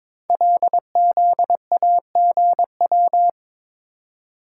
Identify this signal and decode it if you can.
Morse code, heard as a keyed tone, about 22 wpm, 700 Hz. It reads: LZAGW
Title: Code Groups - 5 Letter:Numbers - 22wpm